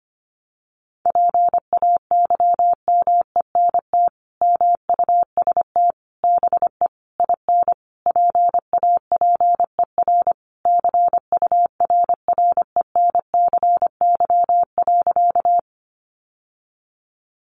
Morse code, heard as a keyed tone, about 25 wpm, 700 Hz. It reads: PAYMENT MUST BE IN PAPER CURRENCY.
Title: Sentences from Top 1500 Words 25wpm